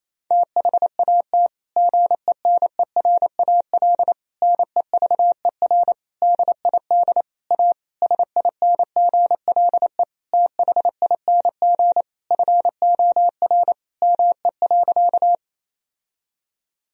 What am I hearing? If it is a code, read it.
Morse code, heard as a keyed tone, about 28 wpm, 700 Hz. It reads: THAT GENERAL NEVER DID A SINGLE THING FOR ME.
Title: Sentences from Top 1000 Words - Review Part 4 - 28wpm